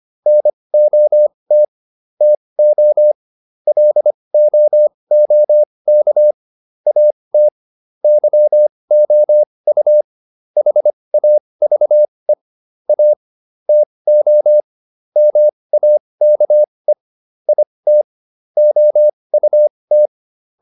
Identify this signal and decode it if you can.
Morse code, heard as a keyed tone, about 25 wpm, spaced as for 20 wpm, 600 Hz. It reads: NOT TO LOOK AT YOU HAVE A TO MAKE IT OUT
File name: WD204a 20